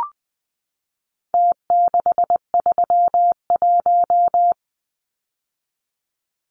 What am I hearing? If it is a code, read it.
Morse code, heard as a keyed tone, about 20 wpm, 700 Hz. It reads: T631
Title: Numbers - Contest Style - 4-Digit 20wpm